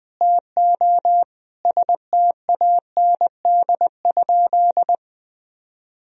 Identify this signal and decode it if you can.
Morse code, heard as a keyed tone, about 20 wpm, 700 Hz. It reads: TO STAND?
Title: Sentences from Top 1000 Words - Review Part 3 - 20wpm